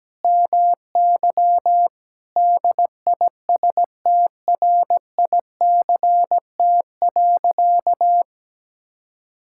Morse code, heard as a keyed tone, about 17 wpm, 700 Hz. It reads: MY DISTRICT.
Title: Sentences from Top 1600 Words 17wpm